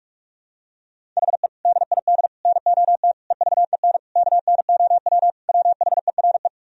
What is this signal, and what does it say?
Morse code, heard as a keyed tone, about 46 wpm, 700 Hz. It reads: HE DID NOT EVEN KNOW WHERE
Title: Sentences from Top 1000 Words - Review Part 1 - 45wpm